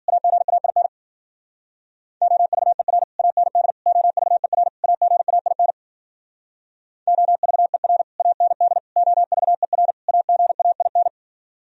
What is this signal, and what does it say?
Morse code, heard as a keyed tone, about 46 wpm, 700 Hz. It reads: AGAIN OVER AND OVER AGAIN OVER AND OVER AGAIN
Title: Speed-Racing - Sets of 4 Words 40wpm